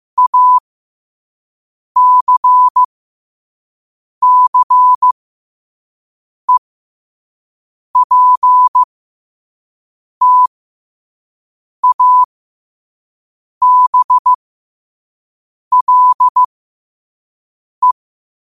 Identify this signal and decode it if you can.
Morse code, heard as a keyed tone, about 15 wpm, spaced as for 5 wpm, 1000 Hz. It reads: ACCEPTABLE
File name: qotdfiles / 5wpm